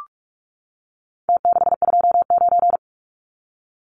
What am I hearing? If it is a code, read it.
Morse code, heard as a keyed tone, about 44 wpm, 700 Hz. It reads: T629
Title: Numbers - Contest Style - 4-Digit 45wpm